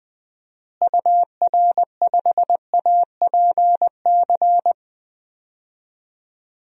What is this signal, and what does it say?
Morse code, heard as a keyed tone, about 20 wpm, 700 Hz. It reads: UR5APC
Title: DXCC-100 Call Signs - 6 Characters - Repeated 2x - 20wpm